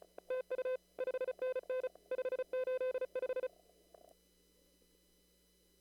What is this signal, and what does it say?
Morse code, heard as a keyed tone, about 32 wpm, 470 Hz. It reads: TU 5NN 585
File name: CW / kt5j wpx cw-op-k5pi